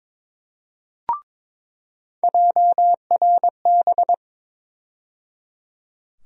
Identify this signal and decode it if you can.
Morse code, heard as a keyed tone, about 22 wpm, 700 Hz. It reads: JRB
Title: USA Call Sign Postfixes - 3 Letters 22wpm